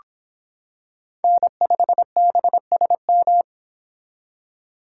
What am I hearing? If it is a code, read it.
Morse code, heard as a keyed tone, about 25 wpm, 700 Hz. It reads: N5BSM